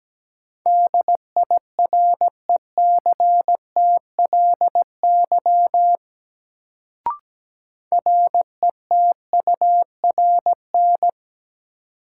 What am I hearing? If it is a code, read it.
Morse code, heard as a keyed tone, about 17 wpm, 700 Hz. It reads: DIRECTLY RETURN
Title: Sentences from Top 800 Words - New Word Drill - 17wpm